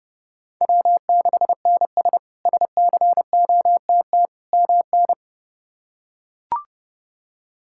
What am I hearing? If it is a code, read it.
Morse code, heard as a keyed tone, about 30 wpm, 700 Hz. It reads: W6NS SCOTT MN
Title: K1USN SST QSO Practice - 30wpm